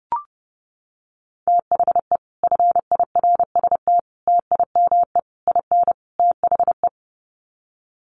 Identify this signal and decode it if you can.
Morse code, heard as a keyed tone, about 30 wpm, 700 Hz. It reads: THE FIRST TIME IN THE
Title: Sets of 5 Words Encore 30wpm